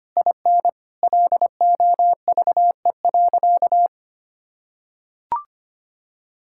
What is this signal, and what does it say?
Morse code, heard as a keyed tone, about 25 wpm, 700 Hz. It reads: IN LOVE.
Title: Sentences from Top 1600 Words 25wpm